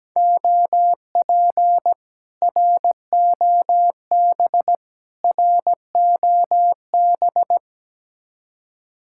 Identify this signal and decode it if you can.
Morse code, heard as a keyed tone, about 17 wpm, 700 Hz. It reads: OP ROB ROB
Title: Speed-Racing Common QSO Phrases - 17wpm